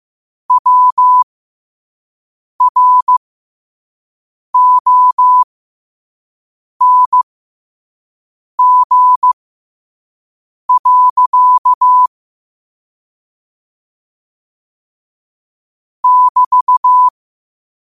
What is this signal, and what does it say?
Morse code, heard as a keyed tone, about 15 wpm, spaced as for 5 wpm, 1000 Hz. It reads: WRONG. <BT>
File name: qotdfiles / 5wpm